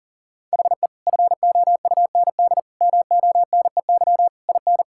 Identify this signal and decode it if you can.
Morse code, heard as a keyed tone, about 37 wpm, 700 Hz. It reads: HE FOUND MONEY IN
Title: Sentences from Top 1000 Words - Review Part 6 - 40wpm